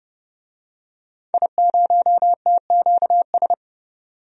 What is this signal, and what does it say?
Morse code, heard as a keyed tone, about 29 wpm, 700 Hz. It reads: I0TQS
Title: DXCC-100 Call Signs - 5 Characters - 30wpm